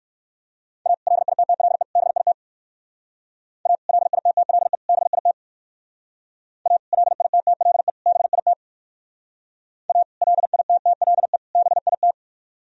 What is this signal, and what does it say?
Morse code, heard as a keyed tone, about 58 wpm, 700 Hz. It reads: A LITTLE BIT A LITTLE BIT A LITTLE BIT A LITTLE BIT